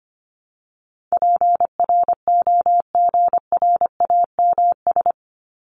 Morse code, heard as a keyed tone, about 25 wpm, 700 Hz. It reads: PROGRAMS
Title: ES:S Suffix - Top 100 Words with Base Words 25wpm